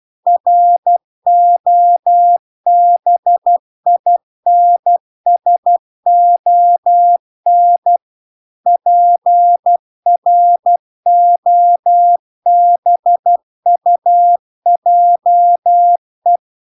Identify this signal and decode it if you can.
Morse code, heard as a keyed tone, about 12 wpm, 700 Hz. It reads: ROBINSON PROBUJE